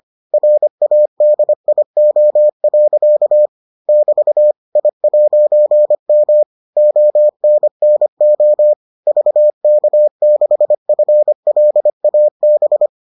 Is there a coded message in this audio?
Yes: Morse code from 0.3 to 12.9 s, heard as a keyed tone, about 25 wpm, 600 Hz. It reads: RADIO. <BT> I'M ONNO VK6FLAB